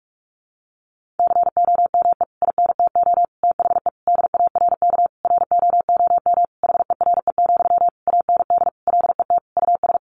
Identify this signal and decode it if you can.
Morse code, heard as a keyed tone, about 44 wpm, 700 Hz. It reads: COME INTO THE DARK ROOM HERE, AND LET US
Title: Sentences from Top 1000 Words 45wpm